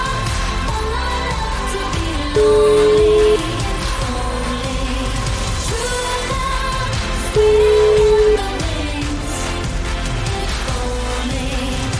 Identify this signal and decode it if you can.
Ringback tone (Europe, 425 Hz), with noise.